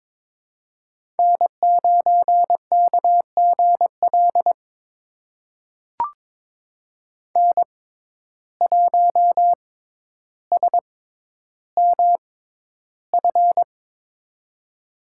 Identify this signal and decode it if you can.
Morse code, heard as a keyed tone, about 22 wpm, spaced as for 11 wpm, 700 Hz. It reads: N9KGL N1SMF
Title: USA Call Signs - 6x Character Spacing - 22wpm